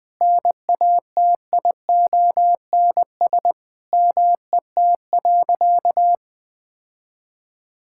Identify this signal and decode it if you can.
Morse code, heard as a keyed tone, about 20 wpm, 700 Hz. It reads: NATIONS MET.
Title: Sentences with 2-Word Phrases - 20wpm